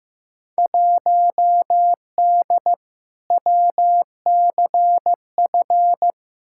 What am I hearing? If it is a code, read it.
Morse code, heard as a keyed tone, about 15 wpm, 700 Hz. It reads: 1D WCF
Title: Field Day Exchanges - 15wpm